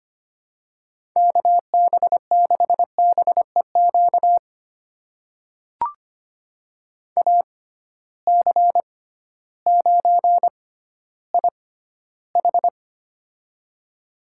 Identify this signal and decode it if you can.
Morse code, heard as a keyed tone, about 25 wpm, spaced as for 13 wpm, 700 Hz. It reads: KB6BEQ AC9IH
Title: USA Call Signs - 6x Character Spacing - 25wpm